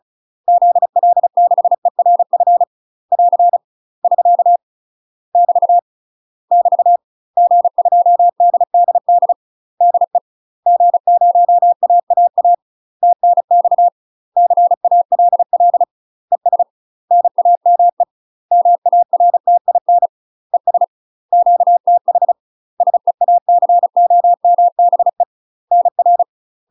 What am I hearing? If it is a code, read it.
Morse code, heard as a keyed tone, about 36 wpm, 700 Hz. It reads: ZL6ERF <AR> <SK> <BT> <BT> G2DDD DE G0AAA TNX CALL ES NAME MARTIN ES QTH SEACOMBE NR
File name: UK MASTER QSOs TRAINING / 35WPM / UK MASTER QSOs 0023